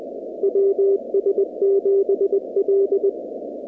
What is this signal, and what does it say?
Morse code, heard as a keyed tone, about 20 wpm, 390 Hz. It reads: WS7L
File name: contests / 2019-cw-wpx-kt5j-k5pi- -2330 0-44